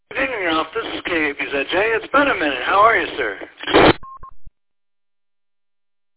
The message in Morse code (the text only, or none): none